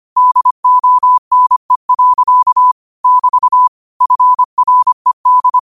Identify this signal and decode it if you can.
Morse code, heard as a keyed tone, about 25 wpm, 1000 Hz. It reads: DONE. <BT> FRED
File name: qotdfiles / 25wpm